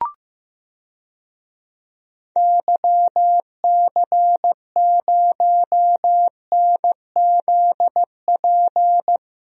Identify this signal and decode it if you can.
Morse code, heard as a keyed tone, about 15 wpm, 700 Hz. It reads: YC0NZP
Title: DXCC-100 Call Signs - 6 Characters - Repeated 2x - 15wpm